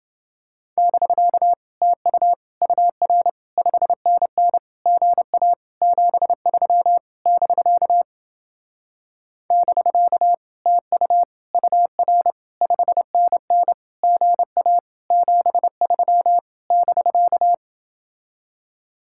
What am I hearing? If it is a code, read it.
Morse code, heard as a keyed tone, about 29 wpm, 700 Hz. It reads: <BK> TU UR 5NN GA 73 <BK> <BK> TU UR 5NN GA 73 <BK>